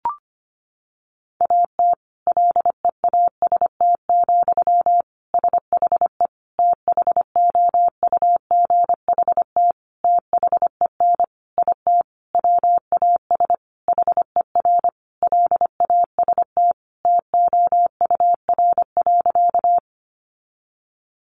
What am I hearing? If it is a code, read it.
Morse code, heard as a keyed tone, about 25 wpm, 700 Hz. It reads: AT LEAST, SHE THOUGHT THEN IT WAS HER LAST TOUR.
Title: Sentences from Top 1400 Words 25wpm